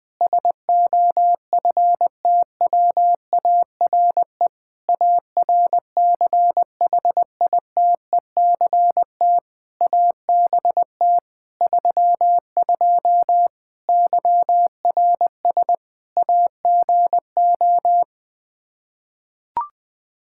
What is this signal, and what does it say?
Morse code, heard as a keyed tone, about 20 wpm, 700 Hz. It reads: SOFTWARE ARCHITECT ABT 32 YRS AGO
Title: Practice QSO - Exchange 3 - More Info - 20wpm